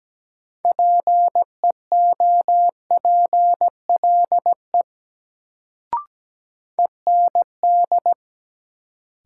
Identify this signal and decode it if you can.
Morse code, heard as a keyed tone, about 17 wpm, 700 Hz. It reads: PEOPLE END
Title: Sentences from Top 200 Words - New Word Drill - 17wpm